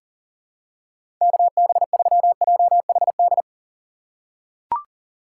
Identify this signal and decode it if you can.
Morse code, heard as a keyed tone, about 40 wpm, 700 Hz. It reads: KB3JHD
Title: USA Call Signs - 6 Characters - 2x Character Spacing - 40wpm